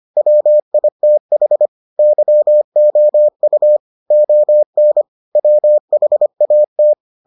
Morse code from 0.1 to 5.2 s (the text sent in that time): WITH YOU ON